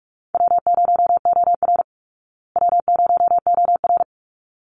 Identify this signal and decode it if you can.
Morse code, heard as a keyed tone, about 44 wpm, 700 Hz. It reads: W0OR W0OR
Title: Call Signs Repeated 3x - 45wpm